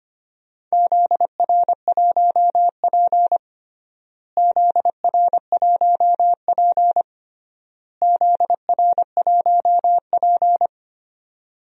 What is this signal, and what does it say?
Morse code, heard as a keyed tone, about 25 wpm, 700 Hz. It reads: ZR1P ZR1P ZR1P
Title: DXCC-100 Call Signs - 4 Characters - Repeated 3x - 25wpm